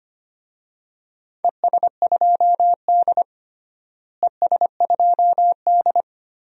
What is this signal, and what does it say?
Morse code, heard as a keyed tone, about 25 wpm, 700 Hz. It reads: ES2D ES2D
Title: DXCC-100 Call Signs - 4 Characters - Repeated 3x - 25wpm